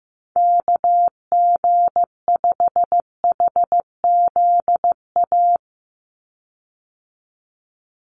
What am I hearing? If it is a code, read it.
Morse code, heard as a keyed tone, about 15 wpm, 700 Hz. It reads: KG5HZA
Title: USA Call Signs 15wpm